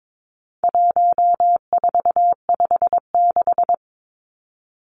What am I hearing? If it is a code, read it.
Morse code, heard as a keyed tone, about 22 wpm, 700 Hz. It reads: 1456
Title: Numbers - 4-Digit 22wpm